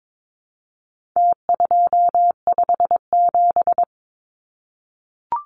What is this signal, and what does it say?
Morse code, heard as a keyed tone, about 22 wpm, 700 Hz. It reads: T257
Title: Numbers - Contest Style - 4-Digit 22wpm